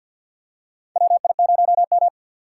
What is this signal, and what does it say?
Morse code, heard as a keyed tone, about 50 wpm, 700 Hz. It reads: WI0M
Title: USA Call Signs - 6x Character Spacing - 50wpm